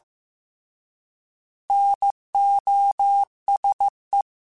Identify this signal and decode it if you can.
Morse code, heard as a keyed tone, about 15 wpm, 800 Hz. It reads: NOSE
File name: Alphabetical / Alphabetical mp3s / Ns @13